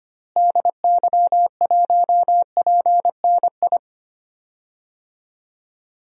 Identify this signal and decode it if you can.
Morse code, heard as a keyed tone, about 25 wpm, 700 Hz. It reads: DY1PNI
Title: DXCC-100 Call Signs - 6 Characters - Repeated 2x - 25wpm